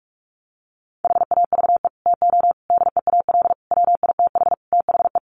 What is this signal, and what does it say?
Morse code, heard as a keyed tone, about 44 wpm, 700 Hz. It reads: HAVE TO DEAL WITH THE
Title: Sets of 5 Words Encore2 45wpm